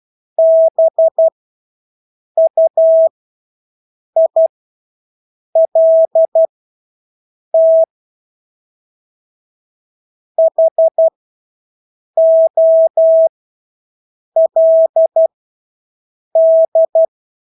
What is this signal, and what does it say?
Morse code, heard as a keyed tone, about 12 wpm, spaced as for 6 wpm, 650 Hz. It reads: BUILT HOLD
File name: Engelske ord 12-6 wpm